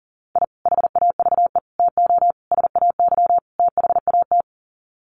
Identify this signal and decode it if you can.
Morse code, heard as a keyed tone, about 40 wpm, 700 Hz. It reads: I HAVE TO SAY THAT
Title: Sets of 5 Words Encore 40wpm